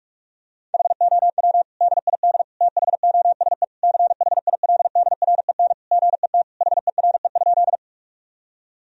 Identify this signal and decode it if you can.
Morse code, heard as a keyed tone, about 46 wpm, 700 Hz. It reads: HOW DID THOSE CHILDREN GET HERE?
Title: Sentences from Top 1000 Words - Review Part 1 - 45wpm